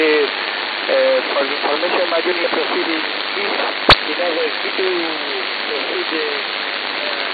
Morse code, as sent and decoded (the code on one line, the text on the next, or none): none